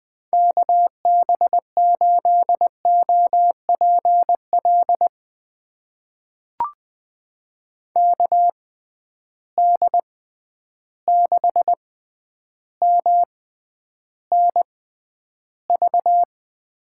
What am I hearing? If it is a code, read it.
Morse code, heard as a keyed tone, about 21 wpm, spaced as for 10 wpm, 700 Hz. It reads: KB8OPL KD6MNV
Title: USA Call Signs - 6 Characters - 6x Character Spacing - 20wpm